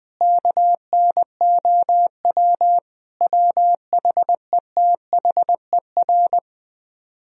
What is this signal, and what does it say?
Morse code, heard as a keyed tone, about 20 wpm, 700 Hz. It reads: KNOW WHETHER
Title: Speed-Racing - Sets of 5 Words 20wpm